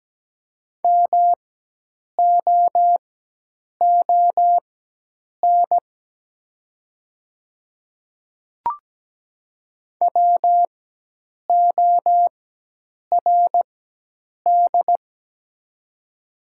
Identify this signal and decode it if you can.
Morse code, heard as a keyed tone, about 17 wpm, spaced as for 8 wpm, 690 Hz. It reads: MOON WORD
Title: 4x Character Spacing - Top 500 Words - 17wpm